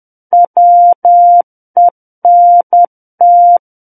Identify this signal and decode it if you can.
Morse code, heard as a keyed tone, about 10 wpm, 700 Hz. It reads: WENT
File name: Book One / 01-Chapter 1 - 10 WPM